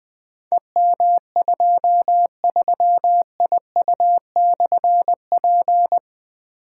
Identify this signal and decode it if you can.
Morse code, heard as a keyed tone, about 20 wpm, 700 Hz. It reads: EM23IU/P
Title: Difficult Call Signs 20wpm